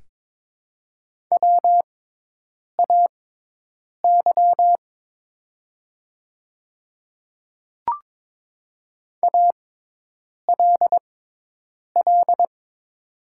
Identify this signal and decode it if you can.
Morse code, heard as a keyed tone, about 22 wpm, spaced as for 8 wpm, 700 Hz. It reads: WAY ALL
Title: 6x Character Spacing - Top 100 Words - 22wpm